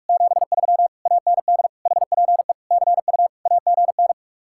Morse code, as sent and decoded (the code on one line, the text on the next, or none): --... ...-- / .- -. -.. / .... .--. / -.-. ..- / .- --. -.
73 AND HPE CU AGN